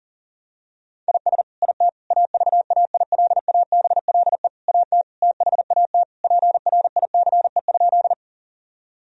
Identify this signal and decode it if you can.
Morse code, heard as a keyed tone, about 40 wpm, 700 Hz. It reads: IS IT AVAILABLE AT THAT PRICE?
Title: Sentences from Top 1000 Words - Review Part 3 - 40wpm